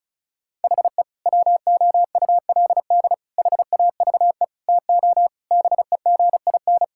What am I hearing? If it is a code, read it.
Morse code, heard as a keyed tone, about 36 wpm, 700 Hz. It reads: HE WOULD HAVE TO BEGIN
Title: Sentences from Top 1000 Words - Review Part 5 - 35wpm